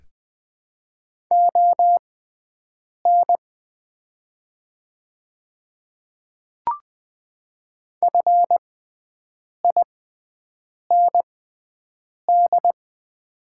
Morse code, heard as a keyed tone, about 20 wpm, spaced as for 7 wpm, 690 Hz. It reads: ON FIND
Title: 6x Character Spacing - Top 100 Words - 20wpm